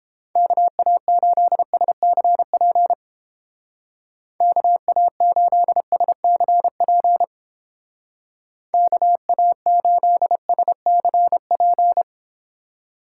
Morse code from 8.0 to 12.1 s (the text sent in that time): KA8SCP